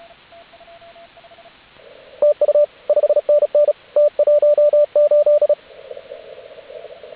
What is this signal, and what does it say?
Morse code, heard as a keyed tone, about 32 wpm, 590 Hz, with radio noise. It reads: TU 5NN T18